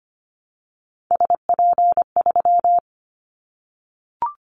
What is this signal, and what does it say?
Morse code, heard as a keyed tone, about 25 wpm, 700 Hz. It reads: SP3
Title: Callsign Prefixes for Top 25 DXCC - Repeated 3x - 25wpm